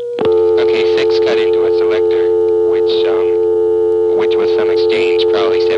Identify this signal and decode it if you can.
Dial tone (North America), 350 + 440 Hz, with speech.
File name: MoreheadCityNCNoTest